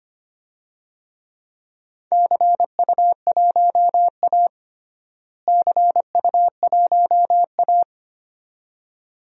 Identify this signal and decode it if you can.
Morse code, heard as a keyed tone, about 25 wpm, 700 Hz. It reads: CU1A CU1A
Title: DXCC-100 Call Signs - 4 Characters - Repeated 2x - 25wpm